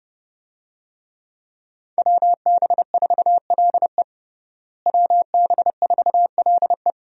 Morse code, heard as a keyed tone, about 32 wpm, 700 Hz. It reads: WB4LE WB4LE
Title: USA Call Signs - 5 Characters - Repeated 2x - 30wpm